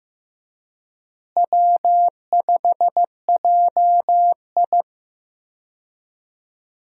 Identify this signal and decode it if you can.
Morse code, heard as a keyed tone, about 14 wpm, 700 Hz. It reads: W5JI
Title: USA Call Signs - 4 Characters - Repeated 3x - 15wpm